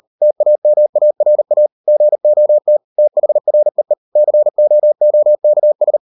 Morse code, heard as a keyed tone, about 38 wpm, 600 Hz. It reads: TAMARA GOT THREE COOKS